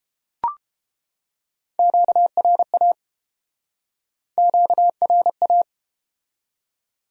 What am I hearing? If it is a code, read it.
Morse code, heard as a keyed tone, about 29 wpm, 700 Hz. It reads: QRA QRA